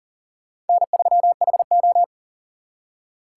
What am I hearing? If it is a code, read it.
Morse code, heard as a keyed tone, about 52 wpm, 700 Hz. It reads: N3HO